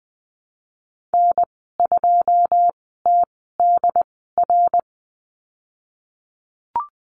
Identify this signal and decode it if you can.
Morse code, heard as a keyed tone, about 20 wpm, spaced as for 15 wpm, 700 Hz. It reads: N2TDR
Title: 186 Lesson 30g - Letter b - Call Signs 20wpm